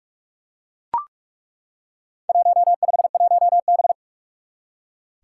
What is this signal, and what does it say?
Morse code, heard as a keyed tone, about 41 wpm, 700 Hz. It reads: 151B
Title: Four Challenging Letter-Number Combinations 45wpm